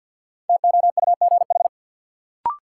Morse code, heard as a keyed tone, about 53 wpm, 700 Hz. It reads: TOUGH
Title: Sentences from Top 1200 Words - New Word Drill Alt - 50wpm